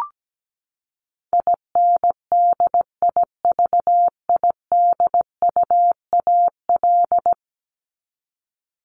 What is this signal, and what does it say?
Morse code, heard as a keyed tone, about 17 wpm, 700 Hz. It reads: INDIVIDUAL
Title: ES:S Suffix - Top 100 Words with Base Words 17wpm